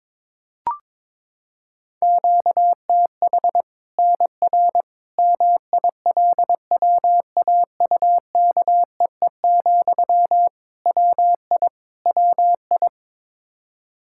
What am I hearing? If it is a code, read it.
Morse code, heard as a keyed tone, about 22 wpm, 700 Hz. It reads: QTH NR MILWAUKEE, WI WI
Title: K1USN SST QSO Practice 22wpm